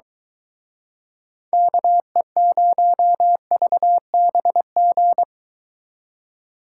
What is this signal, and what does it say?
Morse code, heard as a keyed tone, about 22 wpm, 700 Hz. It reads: KE0VBG